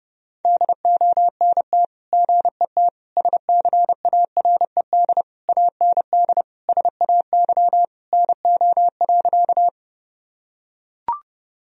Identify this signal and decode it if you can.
Morse code, heard as a keyed tone, about 30 wpm, 700 Hz. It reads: DONT GET SCARED AND SAY NO.
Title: Sentences from Top 1000 Words - Review Part 4 - 30wpm